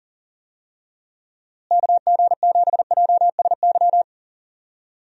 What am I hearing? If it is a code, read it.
Morse code, heard as a keyed tone, about 40 wpm, 700 Hz. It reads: KG7JSY
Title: USA Call Signs - 6 Characters - Repeated 2x - 40wpm